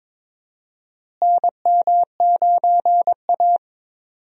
Morse code, heard as a keyed tone, about 22 wpm, 700 Hz. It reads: NM9A
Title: USA Call Signs - 4 Characters - Repeated 3x - 22wpm